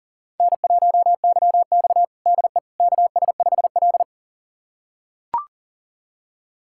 Morse code, heard as a keyed tone, about 40 wpm, 700 Hz. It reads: N1YX DE KS5L
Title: USA Callsign DE Callsign - 4 Characters - 40wpm